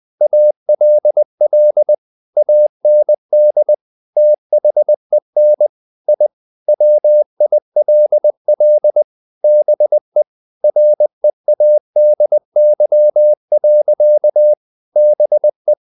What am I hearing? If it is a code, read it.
Morse code, heard as a keyed tone, about 20 wpm, 600 Hz. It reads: ALL AND THEN I WILL BE READY. BE
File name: the old man and the sea / the old man and the sea part 1 20wpm